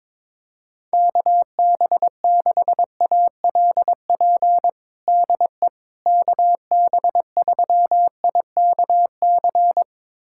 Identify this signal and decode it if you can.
Morse code, heard as a keyed tone, about 22 wpm, 700 Hz. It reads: KB6ALP DE KB3IKC
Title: USA Callsign DE Callsign 22wpm